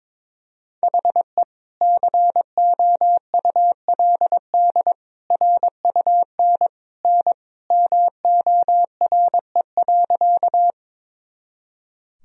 Morse code, heard as a keyed tone, about 22 wpm, 700 Hz. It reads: HE COULD RUN N MORE.
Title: Sentences from Top 300 Words and Abbreviations 22wpm